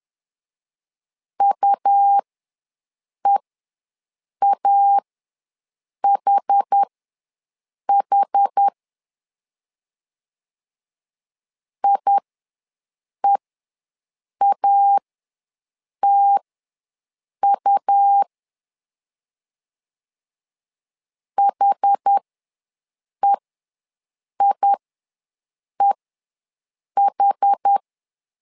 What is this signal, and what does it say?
Morse code, heard as a keyed tone, about 10 wpm, spaced as for 6 wpm, 800 Hz. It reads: UEAHH IEATU HEIEH
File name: LEZIONI / 2 UHDN / 2 UH ws 10wpm 2029